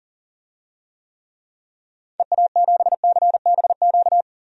Morse code, heard as a keyed tone, about 40 wpm, 700 Hz. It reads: EA7CBQ
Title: DXCC-100 Call Signs - 6 Characters - 40wpm